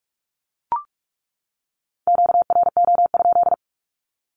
Morse code, heard as a keyed tone, about 44 wpm, 700 Hz. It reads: QRO?
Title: Q-Codes 45wpm